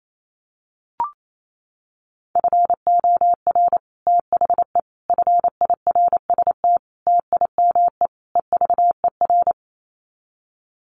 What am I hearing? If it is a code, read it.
Morse code, heard as a keyed tone, about 28 wpm, 700 Hz. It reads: FOR THE FIRST TIME EVER
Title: Sets of 5 Words Encore2 28wpm